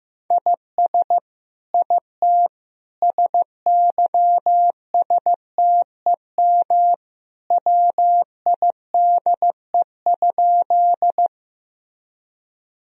Morse code, heard as a keyed tone, about 15 wpm, 700 Hz. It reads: IS IT SYSTEM WIDE?
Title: Sentences from Top 1000 Words - Review Part 3 - 15wpm